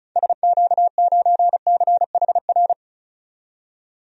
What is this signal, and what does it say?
Morse code, heard as a keyed tone, about 36 wpm, 700 Hz. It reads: SQ9CHR
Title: DXCC-100 Call Signs - 6 Characters - 35wpm